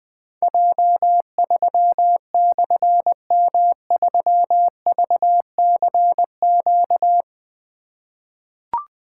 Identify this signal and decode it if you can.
Morse code, heard as a keyed tone, about 20 wpm, 700 Hz. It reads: J3/M3VCQ
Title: Difficult Call Signs 20wpm